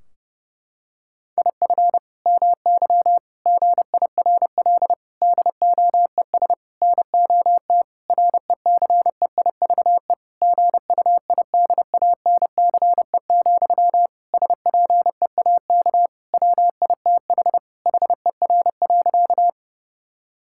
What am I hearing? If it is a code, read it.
Morse code, heard as a keyed tone, about 32 wpm, 700 Hz. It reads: IF MY GIRL DOES NOT RECEIVE GUIDANCE, SPEAK WITH HER.
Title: Sentences from Top 1600 Words 30wpm